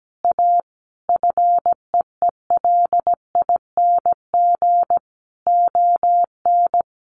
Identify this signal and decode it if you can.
Morse code, heard as a keyed tone, about 17 wpm, 700 Hz. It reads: A FEELING ON
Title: Sentences from Top 1200 Words 17wpm